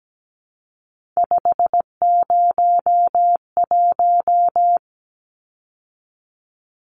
Morse code, heard as a keyed tone, about 17 wpm, 700 Hz. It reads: 501
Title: Numbers - 3-Digit 17wpm